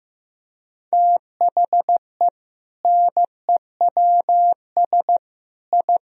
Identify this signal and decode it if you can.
Morse code, heard as a keyed tone, about 15 wpm, 700 Hz. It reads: THE NEWS I
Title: Sentences from Top 1000 Words - Review Part 3 - 15wpm